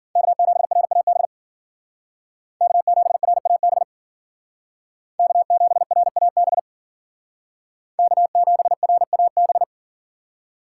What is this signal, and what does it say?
Morse code, heard as a keyed tone, about 52 wpm, 700 Hz. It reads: K7RAB K7RAB K7RAB K7RAB